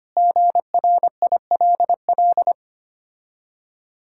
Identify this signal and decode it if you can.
Morse code, heard as a keyed tone, about 26 wpm, 700 Hz. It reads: GRILL
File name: Easy Words - Speed-Racing with Hints - 25wpm